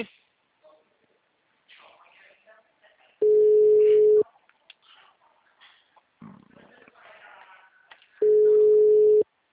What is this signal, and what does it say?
Ringback tone (Europe, 425 Hz), with speech.
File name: out- -2033- - - .16629